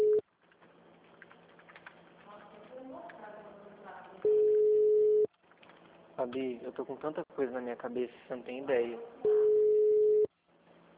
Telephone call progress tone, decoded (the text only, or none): ringback tone